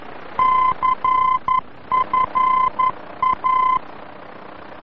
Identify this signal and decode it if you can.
Morse code, heard as a keyed tone, about 11 wpm, 1030 Hz. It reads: CFA
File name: france / mx / CFA-VOR-F2